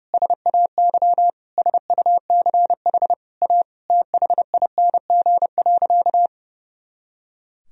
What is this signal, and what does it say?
Morse code, heard as a keyed tone, about 32 wpm, 700 Hz. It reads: SAY SUCH A THING.
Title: Sentences from Top 200 Words and Abbreviations 30wpm